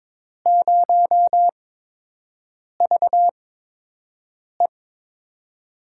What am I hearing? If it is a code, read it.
Morse code, heard as a keyed tone, about 22 wpm, spaced as for 6 wpm, 700 Hz. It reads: 0VE
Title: USA Call Signs - 6 Characters - 8x Character Spacing - 22wpm